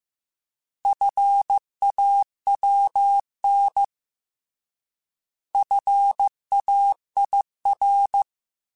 Morse code, heard as a keyed tone, about 15 wpm, 800 Hz. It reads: FAWN FAIR
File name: Alphabetical / Alphabetical mp3s / Fs @13